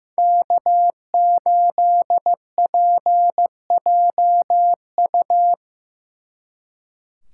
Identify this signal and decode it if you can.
Morse code, heard as a keyed tone, about 18 wpm, 700 Hz. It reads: K8PJU
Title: Speed Racing - USA 1x3 Call Signs 15wpm